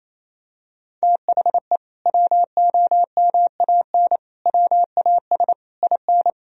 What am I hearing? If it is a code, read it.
Morse code, heard as a keyed tone, about 28 wpm, 700 Hz. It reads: THE WOMAN WAS IN
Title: Sentences from Top 1000 Words - Review Part 3 - 28wpm